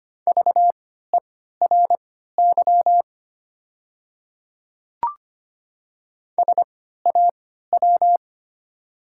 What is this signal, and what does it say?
Morse code, heard as a keyed tone, about 25 wpm, spaced as for 14 wpm, 700 Hz. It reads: VERY SAW
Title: 3x Character Spacing - Top 500 Words - 25wpm